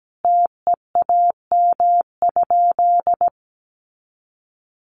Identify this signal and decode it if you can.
Morse code, heard as a keyed tone, about 17 wpm, 700 Hz. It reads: TEAM?
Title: Sentences from Top 1500 Words 17wpm